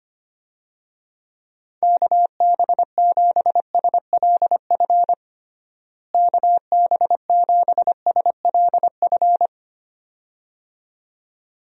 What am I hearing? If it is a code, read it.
Morse code, heard as a keyed tone, about 25 wpm, 700 Hz. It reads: KB7SLF KB7SLF
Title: USA Call Signs - 6 Characters - Repeated 2x - 25wpm